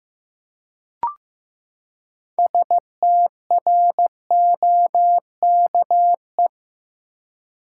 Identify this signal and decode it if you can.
Morse code, heard as a keyed tone, about 15 wpm, 700 Hz. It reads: STROKE
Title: Six Letter Words - 15wpm